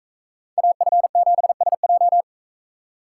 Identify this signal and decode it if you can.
Morse code, heard as a keyed tone, about 40 wpm, 700 Hz. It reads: AF7SJ